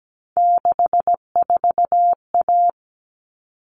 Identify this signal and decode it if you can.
Morse code, heard as a keyed tone, about 17 wpm, 700 Hz. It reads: 64A
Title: All Cut Numbers - 3-Digit 17wpm